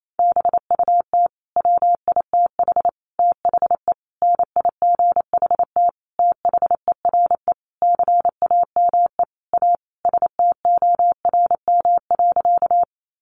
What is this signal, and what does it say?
Morse code, heard as a keyed tone, about 28 wpm, 700 Hz. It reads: BUT WITH THE NIGHT THERE CAME A STORM.
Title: Sentences from Top 1500 Words 28wpm